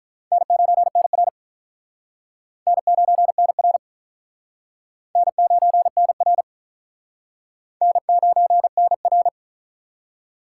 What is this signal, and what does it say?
Morse code, heard as a keyed tone, about 46 wpm, 700 Hz. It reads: N9NR N9NR N9NR N9NR